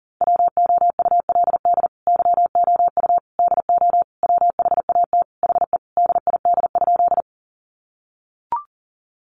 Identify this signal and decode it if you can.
Morse code, heard as a keyed tone, about 40 wpm, 700 Hz. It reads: WOULD YOU DO WHAT HE DID?
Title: Sentences from Top 100 Words 40wpm